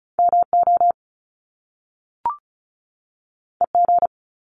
Morse code, heard as a keyed tone, about 36 wpm, 700 Hz. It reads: MO EG
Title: Top Two Letter Combinations in Words 35wpm